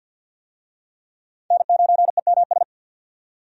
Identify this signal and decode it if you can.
Morse code, heard as a keyed tone, about 52 wpm, 700 Hz. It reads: N9EMS
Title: USA Call Signs - 5 Characters - 50wpm